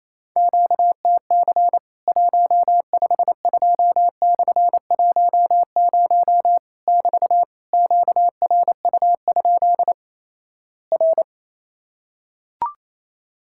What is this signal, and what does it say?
Morse code, heard as a keyed tone, about 27 wpm, 700 Hz. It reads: QTC 152/10 <BT> QRU? R
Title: Worked All Europe DX Contest - QTC Exchange - 28wpm